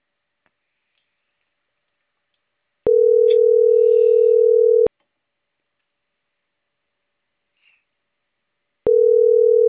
Ringback tone (North America), 440 + 480 Hz.